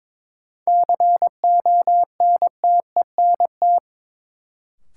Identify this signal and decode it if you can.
Morse code, heard as a keyed tone, about 21 wpm, 700 Hz. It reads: CONTENT
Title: Sentences from Top 900 Words - New Word Drill - 22wpm